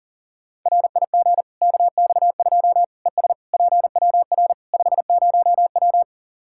Morse code, heard as a keyed tone, about 40 wpm, 700 Hz. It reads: RIG KX2 ES PWR 50W
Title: Practice QSO - Exchange 2 - Info - 40wpm